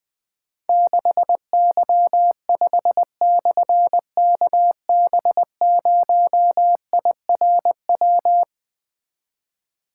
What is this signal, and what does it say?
Morse code, heard as a keyed tone, about 20 wpm, 700 Hz. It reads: 6Y5/KB0IRW
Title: Difficult Call Signs 20wpm